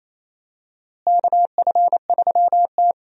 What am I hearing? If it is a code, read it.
Morse code, heard as a keyed tone, about 28 wpm, 700 Hz. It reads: KF3T
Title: USA Call Signs - 4 Characters - Repeated 3x - 28wpm